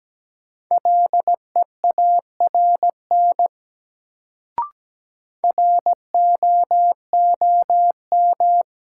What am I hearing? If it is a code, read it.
Morse code, heard as a keyed tone, about 17 wpm, 700 Hz. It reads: LEARN ROOM
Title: Sentences from Top 300 Words - New Word Drill - 17wpm